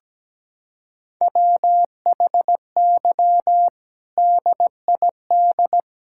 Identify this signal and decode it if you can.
Morse code, heard as a keyed tone, about 17 wpm, 700 Hz. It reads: WHY DID
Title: Sentences from Top 1000 Words - Review Part 6 - 17wpm